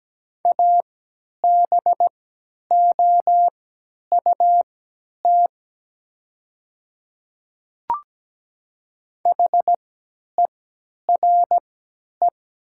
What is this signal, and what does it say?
Morse code, heard as a keyed tone, about 17 wpm, spaced as for 10 wpm, 690 Hz. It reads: ABOUT HERE
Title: 3x Character Spacing - Top 300 Words - 17wpm